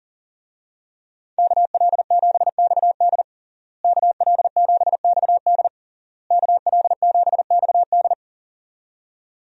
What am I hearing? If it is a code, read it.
Morse code, heard as a keyed tone, about 40 wpm, 700 Hz. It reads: KL7XD KL7XD KL7XD
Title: DXCC-100 Call Signs - 5 Characters - Repeated 3x - 40wpm